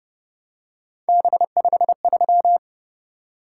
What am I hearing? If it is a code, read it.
Morse code, heard as a keyed tone, about 30 wpm, 700 Hz. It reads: B53
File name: Speed-Racing - Three Letter-Number Combinations - 25wpm